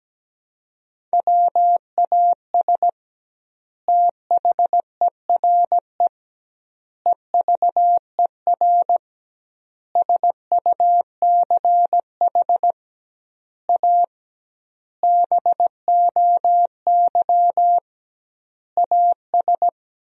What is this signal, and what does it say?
Morse code, heard as a keyed tone, about 17 wpm, 700 Hz. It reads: WAS THERE EVER SUCH A BOY AS
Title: 2x Word-Spacing - Sentences from Top 400 Words - 17wpm